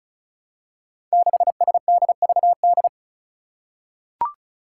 Morse code, heard as a keyed tone, about 34 wpm, 700 Hz. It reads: 6SDVD
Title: Five Challenging Letter-Number Combinations 35wpm